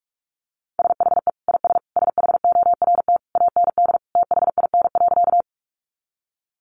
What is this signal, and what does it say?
Morse code, heard as a keyed tone, about 44 wpm, 700 Hz. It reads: SHE IS SHORT AND THIN.
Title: Sentences from Top 1200 Words 45wpm